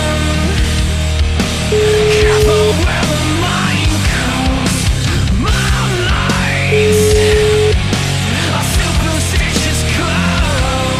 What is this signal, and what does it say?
Ringback tone (Europe, 425 Hz), with noise.